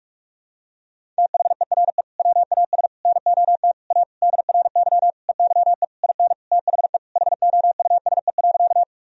Morse code, heard as a keyed tone, about 46 wpm, 700 Hz. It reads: THERE WAS NOT A DRY EYE IN THE HOUSE.
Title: Sentences from Top 1000 Words - Review Part 3 - 45wpm